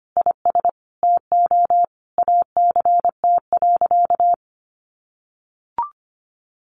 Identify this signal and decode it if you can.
Morse code, heard as a keyed tone, about 25 wpm, 700 Hz. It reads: IS TO ACT.
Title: Sentences from Top 1200 Words 25wpm